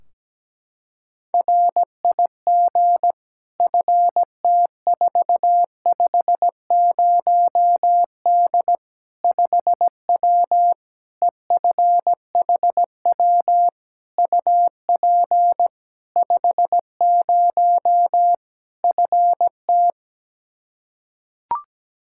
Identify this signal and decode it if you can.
Morse code, heard as a keyed tone, about 17 wpm, 700 Hz. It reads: RIG FT450D 5W EFHW UP 50 FT
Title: Practice QSO - Exchange 2 - Info - 17wpm